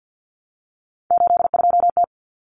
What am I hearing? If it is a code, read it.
Morse code, heard as a keyed tone, about 50 wpm, 700 Hz. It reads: 82T
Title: Common Cut Numbers - 3-Digit 50wpm